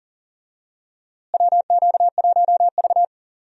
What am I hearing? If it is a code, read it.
Morse code, heard as a keyed tone, about 40 wpm, 700 Hz. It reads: WQ1V
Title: USA Call Signs - 4 Characters - Repeated 3x - 40wpm